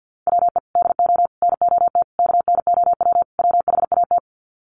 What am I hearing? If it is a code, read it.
Morse code, heard as a keyed tone, about 50 wpm, 700 Hz. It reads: WE DO NOT KNOW WHAT
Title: Sets of 5 Words 50wpm